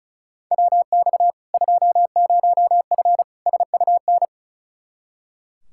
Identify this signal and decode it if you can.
Morse code, heard as a keyed tone, about 34 wpm, 700 Hz. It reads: WX 20F SUN
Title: Practice QSO - Exchange 2 - Info - 35wpm